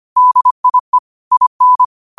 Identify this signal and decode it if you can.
Morse code, heard as a keyed tone, about 25 wpm, 1000 Hz. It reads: DIE IN
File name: qotdfiles / 25wpm